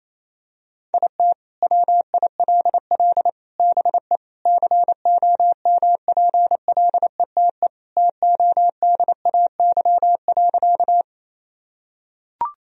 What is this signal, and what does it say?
Morse code, heard as a keyed tone, about 28 wpm, 700 Hz. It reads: IT WILL BE COMPLETE TODAY.
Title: Sentences from Top 1000 Words - Review Part 2 - 28wpm